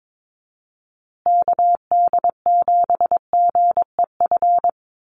Morse code, heard as a keyed tone, about 22 wpm, 700 Hz. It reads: KD7GEF
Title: USA Call Signs Repeated 3x 22wpm